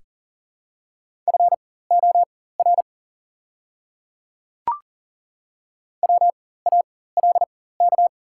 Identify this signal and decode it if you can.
Morse code, heard as a keyed tone, about 40 wpm, spaced as for 19 wpm, 700 Hz. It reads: FOR WALK
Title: 4x Character Spacing - Top 500 Words - 40wpm